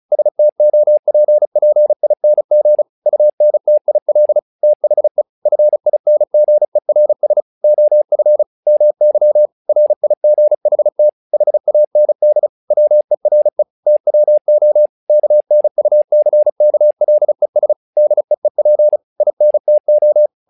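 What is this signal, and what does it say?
Morse code, heard as a keyed tone, about 36 wpm, 600 Hz. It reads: STOPPING UNTIL THE FINGERS OF MY RIGHT HAND WERE TWO KNUCKLES DEEP INTO
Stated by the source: Morse practice text "Tower Trouble" sent at 35 WPM, from the SKCC - Straight Key Century Club learning center audio files.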